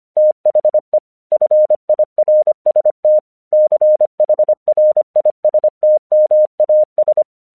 Morse code, heard as a keyed tone, about 25 wpm, 610 Hz. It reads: THE FIRST CHRISTMAS
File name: 1 English 25wpm / AQC05